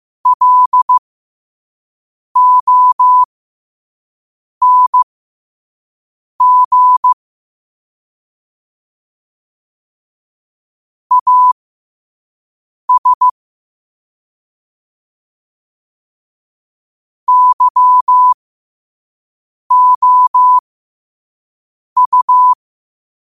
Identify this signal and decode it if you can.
Morse code, heard as a keyed tone, about 15 wpm, spaced as for 5 wpm, 1000 Hz. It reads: LONG AS YOU
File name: 5wpm